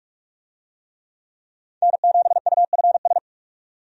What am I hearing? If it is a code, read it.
Morse code, heard as a keyed tone, about 46 wpm, 700 Hz. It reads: N7UFS
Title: USA Call Signs - 5 Characters - 45wpm